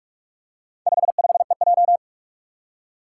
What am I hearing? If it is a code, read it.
Morse code, heard as a keyed tone, about 46 wpm, 700 Hz. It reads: 55EJ
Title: Four Challenging Letter-Number Combinations 45wpm